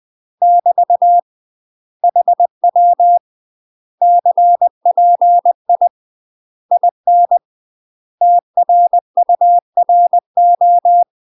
Morse code, heard as a keyed tone, about 20 wpm, 700 Hz. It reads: <BT> HW CPI IN TRURO
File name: UK MASTER QSOs TRAINING / 20WPM / UK MASTER QSOs 0073